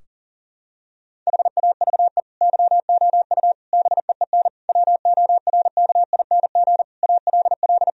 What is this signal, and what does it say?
Morse code, heard as a keyed tone, about 40 wpm, 700 Hz. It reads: HAVE YOU BEEN WORKING ALL
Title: Sentences from Top 1000 Words - Review Part 4 - 40wpm